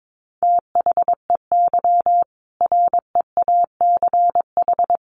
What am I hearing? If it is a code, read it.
Morse code, heard as a keyed tone, about 22 wpm, 700 Hz. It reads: THEY REACH
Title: Easy Phrases 22wpm